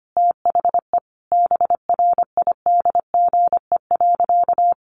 Morse code, heard as a keyed tone, about 25 wpm, 700 Hz. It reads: THE BRIDGE.
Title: Sentences from Top 1400 Words 25wpm